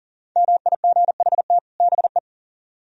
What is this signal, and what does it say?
Morse code, heard as a keyed tone, about 48 wpm, 700 Hz. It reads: MIGHT BE
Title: Speed-Racing - Sets of 2 Words 40wpm